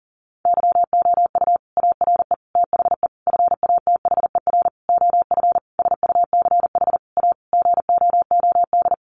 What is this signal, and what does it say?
Morse code, heard as a keyed tone, about 37 wpm, 700 Hz. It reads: YOU ARE THE FATHER OF SUCH A GOOD
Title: Sentences from Top 1500 Words 40wpm